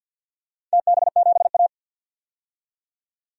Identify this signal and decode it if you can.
Morse code, heard as a keyed tone, about 49 wpm, 700 Hz. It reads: TB7A
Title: DXCC-100 Call Signs - 4 Characters - 50wpm